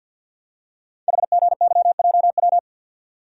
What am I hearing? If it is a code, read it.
Morse code, heard as a keyed tone, about 48 wpm, 700 Hz. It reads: HGYJW
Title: Code Groups - 5 Letters - 50wpm